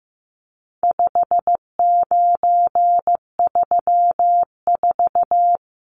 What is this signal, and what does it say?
Morse code, heard as a keyed tone, about 15 wpm, 700 Hz. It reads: 5934
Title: Numbers - 4-Digit 15wpm